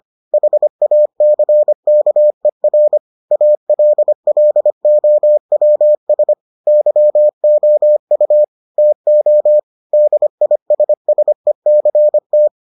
Morse code, heard as a keyed tone, about 25 wpm, 600 Hz. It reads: HACKER ALLOWS YOU TO DISSECT